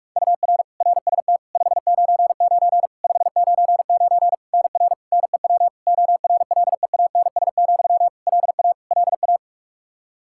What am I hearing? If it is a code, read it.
Morse code, heard as a keyed tone, about 41 wpm, 700 Hz. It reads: UR RST 599 599 NR NEW ORLEANS, LA LA
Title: Practice QSO - Exchange 1 - Beginning - 45wpm